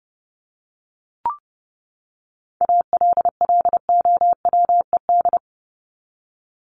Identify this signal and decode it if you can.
Morse code, heard as a keyed tone, about 30 wpm, 700 Hz. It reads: ALLOWED
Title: ED Suffix - Top 100 Words with Base Words 30wpm